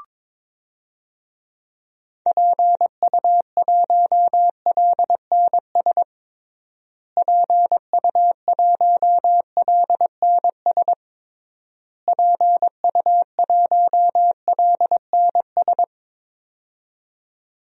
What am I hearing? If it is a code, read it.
Morse code, heard as a keyed tone, about 22 wpm, 700 Hz. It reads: PU1LNS PU1LNS PU1LNS
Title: DXCC-100 Call Signs - 6 Characters - Repeated 3x - 22wpm